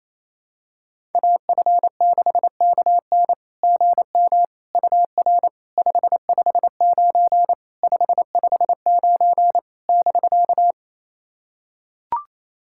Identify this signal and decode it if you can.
Morse code, heard as a keyed tone, about 28 wpm, 690 Hz. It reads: AF6KN GM UR 559 559 <BK>
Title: SOTA QSO Practice 28wpm